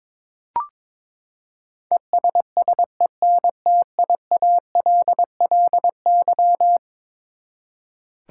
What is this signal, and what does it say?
Morse code, heard as a keyed tone, about 21 wpm, 700 Hz. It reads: ESSENTIALLY
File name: practice / 22wpm / Eleven Letter Words - 22wpm